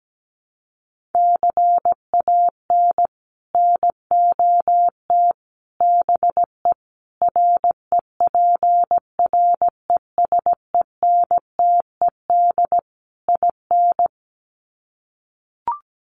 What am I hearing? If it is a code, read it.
Morse code, heard as a keyed tone, about 17 wpm, 700 Hz. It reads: CAN NOT BE REPRESENTED IN
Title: Sets of 5 Words 17wpm